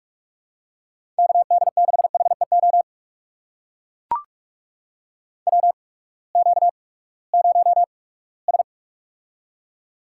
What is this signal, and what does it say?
Morse code, heard as a keyed tone, about 44 wpm, spaced as for 22 wpm, 700 Hz. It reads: KD6HEO WQ0S
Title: USA Call Signs - 8x Character Spacing - 45wpm